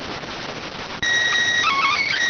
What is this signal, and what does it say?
SSTV (slow-scan television), PD 290.